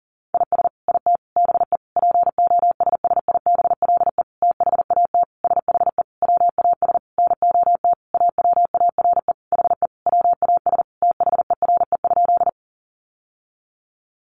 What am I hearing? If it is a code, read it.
Morse code, heard as a keyed tone, about 40 wpm, 700 Hz. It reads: IS IT BE POSSIBLE THAT SHE WAS NOT AWARE HE WAS THERE?
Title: Sentences from Top 1200 Words 40wpm